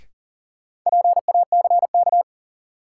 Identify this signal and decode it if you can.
Morse code, heard as a keyed tone, about 38 wpm, 700 Hz. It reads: PACK
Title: Sentences from Top 1300 Words - New Word Drill - 40wpm